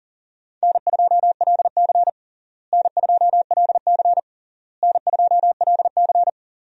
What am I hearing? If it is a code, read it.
Morse code, heard as a keyed tone, about 39 wpm, 700 Hz. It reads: N2LC N2LC N2LC
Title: USA Call Signs - 4 Characters - Repeated 3x - 40wpm